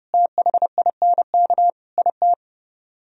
Morse code, heard as a keyed tone, about 30 wpm, 700 Hz. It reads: THINK IT